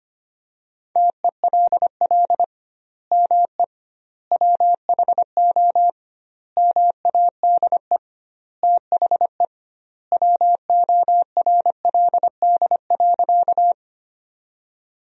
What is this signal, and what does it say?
Morse code, heard as a keyed tone, about 25 wpm, 700 Hz. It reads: TELL ME WHO MADE THE WORLD.
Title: 2x Word-Spacing - Sentences from Top 200 Words - 25wpm